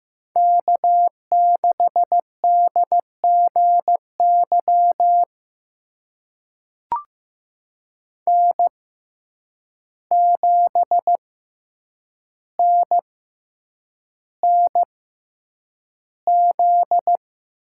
Morse code, heard as a keyed tone, about 15 wpm, spaced as for 8 wpm, 700 Hz. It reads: K6DGY N7NNZ
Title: USA Call Signs - 5 Characters - 6x Character Spacing - 15wpm